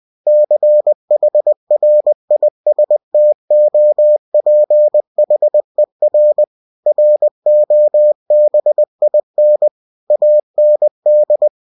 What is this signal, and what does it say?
Morse code, heard as a keyed tone, about 20 wpm, 600 Hz. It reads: CHRISTOPHER ROBIN AND
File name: cw-books / winnie-the-pooh / winnie-20wpm-page-0091